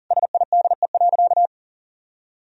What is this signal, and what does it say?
Morse code, heard as a keyed tone, about 40 wpm, 700 Hz. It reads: SIDE.
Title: Sentences from Top 400 Words and Abbreviations 40wpm